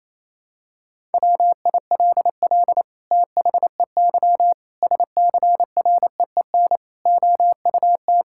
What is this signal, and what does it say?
Morse code, heard as a keyed tone, about 28 wpm, 700 Hz. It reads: WILL THEY SCREEN OUT
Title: Sentences from Top 1000 Words - Review Part 6 - 28wpm